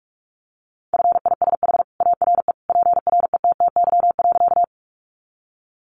Morse code, heard as a keyed tone, about 44 wpm, 700 Hz. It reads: FISH ARE PRETTY.
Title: Easy Phrases 45wpm